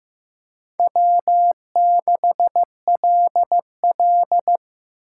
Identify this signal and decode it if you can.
Morse code, heard as a keyed tone, about 15 wpm, 700 Hz. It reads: W6LL
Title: USA Call Signs - 4 Characters - 6x Character Spacing - 15wpm